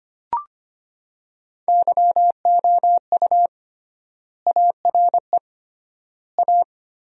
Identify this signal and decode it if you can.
Morse code, heard as a keyed tone, about 25 wpm, 700 Hz. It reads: YOU ARE A
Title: 3x Word-Spacing - Sentences from Top 200 Words - 25wpm